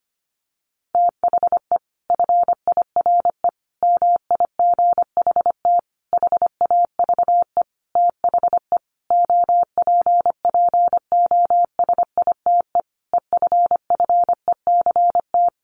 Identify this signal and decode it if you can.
Morse code, heard as a keyed tone, about 25 wpm, 700 Hz. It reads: THE FIRE MIGHT HAVE THE OPPOSITE EFFECT
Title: Sentences from Top 1300 Words 25wpm